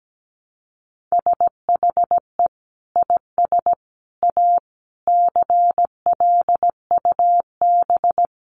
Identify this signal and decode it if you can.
Morse code, heard as a keyed tone, about 17 wpm, 700 Hz. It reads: SHE IS A CLUB
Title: Sentences from Top 1200 Words 17wpm